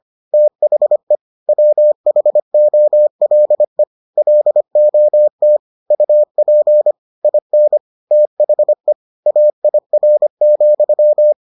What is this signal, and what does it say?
Morse code, heard as a keyed tone, about 25 wpm, 600 Hz. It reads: THE WHOLE LOT UP IN THE AIR,